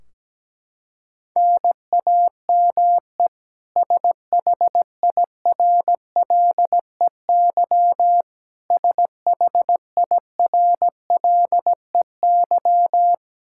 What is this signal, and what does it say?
Morse code, heard as a keyed tone, about 17 wpm, 700 Hz. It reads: NAME SHIRLEY SHIRLEY
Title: K1USN SST QSO Practice 17wpm